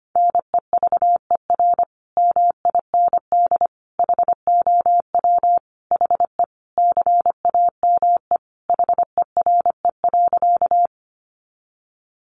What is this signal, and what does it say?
Morse code, heard as a keyed tone, about 25 wpm, 700 Hz. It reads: NEVER MIND HOW HE CAME HERE.
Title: Sentences from Top 500 Words 25wpm